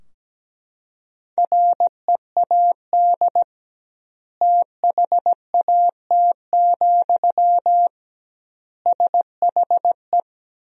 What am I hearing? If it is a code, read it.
Morse code, heard as a keyed tone, about 17 wpm, 700 Hz. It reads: READ THAT, SHE
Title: 2x Word-Spacing - Sentences from Top 200 Words - 17wpm